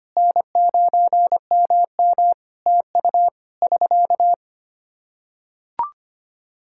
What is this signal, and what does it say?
Morse code, heard as a keyed tone, about 25 wpm, 700 Hz. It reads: N9MM TU <SK>
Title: Practice QSO - Exchange 4 - Ending - 25wpm